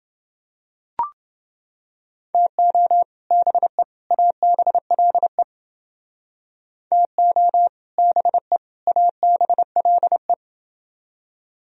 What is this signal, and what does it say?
Morse code, heard as a keyed tone, about 29 wpm, 700 Hz. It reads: TO BE ABLE TO BE ABLE